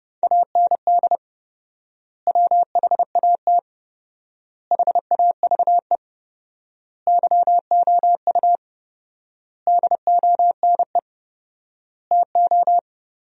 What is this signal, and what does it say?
Morse code, heard as a keyed tone, about 30 wpm, 700 Hz. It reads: AND WHAT HAVE YOU DONE TO
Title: 4x Word-Spacing - Sentences from Top 500 Words - 30wpm